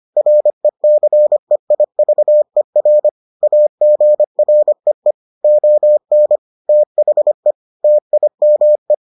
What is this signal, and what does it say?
Morse code, heard as a keyed tone, about 25 wpm, 600 Hz. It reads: RECEIVER AGREE ON THE TIME